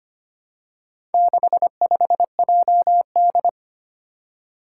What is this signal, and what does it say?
Morse code, heard as a keyed tone, about 25 wpm, 700 Hz. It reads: 65JD
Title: Four Challenging Letter-Number Combinations 25wpm